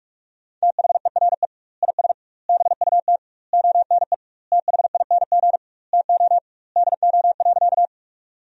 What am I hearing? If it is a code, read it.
Morse code, heard as a keyed tone, about 47 wpm, 700 Hz. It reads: THERE IS BUT ONE THING TO DO.
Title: 2x Word-Spacing - Sentences from Top 100 Words - 45wpm